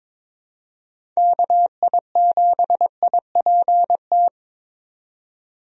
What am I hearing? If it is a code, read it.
Morse code, heard as a keyed tone, about 21 wpm, 700 Hz. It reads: KI7IPT
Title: USA Call Signs - 6 Characters - 22wpm